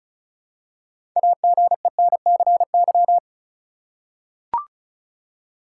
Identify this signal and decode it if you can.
Morse code, heard as a keyed tone, about 36 wpm, 700 Hz. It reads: AGENCY
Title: Six Letter Words - 35wpm